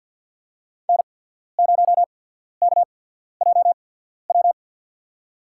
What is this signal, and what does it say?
Morse code, heard as a keyed tone, about 53 wpm, spaced as for 14 wpm, 700 Hz. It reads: N0KJW
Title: USA Call Signs - 5 Characters - 8x Character Spacing - 50wpm